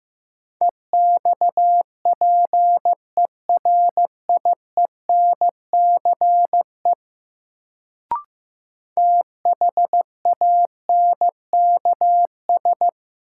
Morse code, heard as a keyed tone, about 15 wpm, 700 Hz. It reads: EXPERIENCE THANKS
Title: Sentences from Top 600 Words - New Word Drill - 15wpm